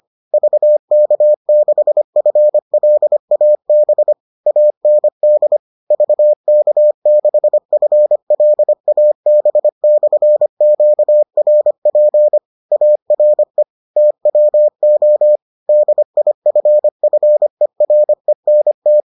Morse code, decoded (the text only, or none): VK6FLAB AND VK6FLAB/QRP ARE TWO DIFFERENT